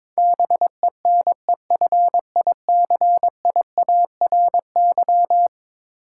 Morse code, heard as a keyed tone, about 21 wpm, 700 Hz. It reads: BENEFICIARY
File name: practice / 22wpm / Eleven Letter Words - 22wpm